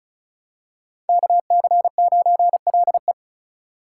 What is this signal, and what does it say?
Morse code, heard as a keyed tone, about 36 wpm, 700 Hz. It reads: KC9LE
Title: USA Call Signs - 3x Character Spacing - 35wpm